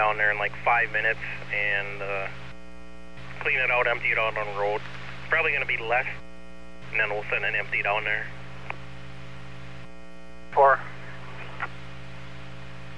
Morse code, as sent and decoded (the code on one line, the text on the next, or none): none